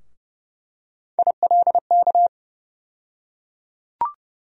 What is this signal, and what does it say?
Morse code, heard as a keyed tone, about 30 wpm, 700 Hz. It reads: ILK
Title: USA Call Sign Postfixes - 3 Letters 30wpm